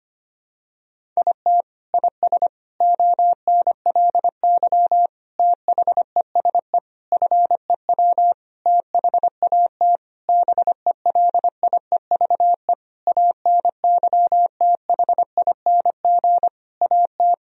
Morse code, heard as a keyed tone, about 25 wpm, 700 Hz. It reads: IT IS ONLY THESE FEW THAT BELIEVE ANYTHING AT
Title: Sentences from Top 1000 Words - Review Part 1 - 25wpm